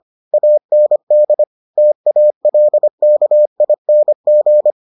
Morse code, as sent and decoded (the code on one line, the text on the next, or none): .- -. -.. / - .- .-.. -.- .. -. --.
AND TALKING